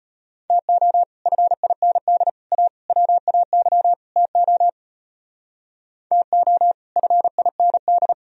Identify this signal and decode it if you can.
Morse code, heard as a keyed tone, about 31 wpm, 700 Hz. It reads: TO FIND A WAY TO TO FIND
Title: Speed-Racing - Sets of 5 Words 25wpm